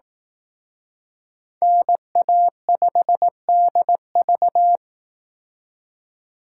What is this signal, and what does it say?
Morse code, heard as a keyed tone, about 18 wpm, 700 Hz. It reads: NA5DV
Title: Speed Racing - USA 2x2 Call Signs 15wpm